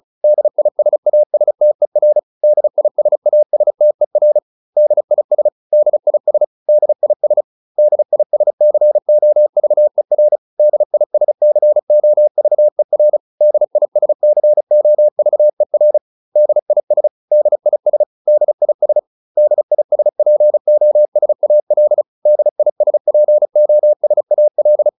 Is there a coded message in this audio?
Yes: Morse code from 0.2 to 24.9 s, heard as a keyed tone, about 34 wpm, 600 Hz. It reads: DISASTER DISASTER DIS DIS DIS DISCOVER DISCOVER DISCOVER DIS DIS DIS DISPOSAL DISPOSAL